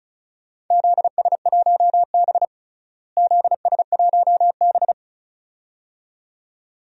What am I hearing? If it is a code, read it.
Morse code, heard as a keyed tone, about 36 wpm, 700 Hz. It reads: ZS1B ZS1B
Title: DXCC-100 Call Signs - 4 Characters - Repeated 2x - 35wpm